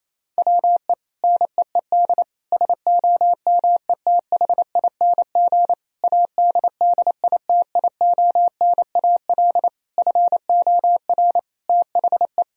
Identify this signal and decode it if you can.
Morse code, heard as a keyed tone, about 28 wpm, 700 Hz. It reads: WE NEED SOMETHING ADDITIONAL FOR THE
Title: Sentences from Top 1000 Words - Review Part 4 - 28wpm